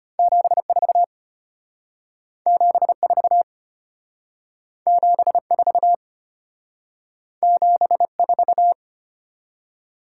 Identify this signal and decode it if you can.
Morse code, heard as a keyed tone, about 31 wpm, 700 Hz. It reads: 74 74 74 74
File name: practice / 25wpm / Speed-Racing - Two Letter-Number Combinations - 25wpm